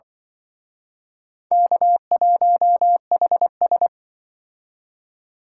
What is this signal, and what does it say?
Morse code, heard as a keyed tone, about 24 wpm, 700 Hz. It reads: K1HS
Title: Speed Racing - USA 1x2 Call Signs 20wpm